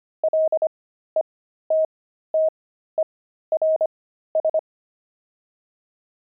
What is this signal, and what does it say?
Morse code, heard as a keyed tone, about 25 wpm, spaced as for 13 wpm, 630 Hz. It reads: LETTERS